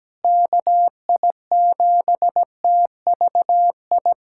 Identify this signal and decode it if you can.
Morse code, heard as a keyed tone, about 17 wpm, 700 Hz. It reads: KI7TVI
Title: USA Callsign DE Callsign - 6 Characters - 17wpm